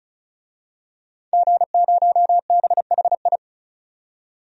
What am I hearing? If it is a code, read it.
Morse code, heard as a keyed tone, about 36 wpm, 700 Hz. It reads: G0BHI
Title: DXCC-100 Call Signs - 5 Characters - 35wpm